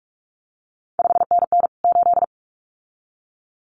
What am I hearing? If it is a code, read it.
Morse code, heard as a keyed tone, about 46 wpm, 700 Hz. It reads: 5NN 8
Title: CQ WW DX Contest Exchange - 45wpm